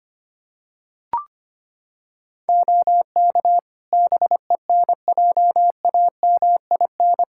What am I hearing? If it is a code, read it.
Morse code, heard as a keyed tone, about 25 wpm, 700 Hz. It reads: OK BENJAMIN
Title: Practice QSO - Exchange 4 - Ending - 25wpm